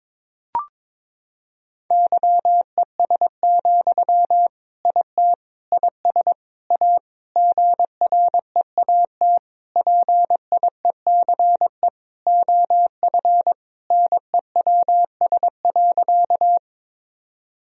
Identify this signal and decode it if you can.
Morse code, heard as a keyed tone, about 22 wpm, 700 Hz. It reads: YES, IT IS A GREAT PIECE OF NEWS.
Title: Sentences from Top 1000 Words - Review Part 2 - 22wpm